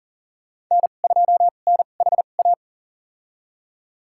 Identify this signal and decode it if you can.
Morse code, heard as a keyed tone, about 40 wpm, spaced as for 29 wpm, 700 Hz. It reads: N2NHA
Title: USA Call Signs - 2x Character Spacing - 40wpm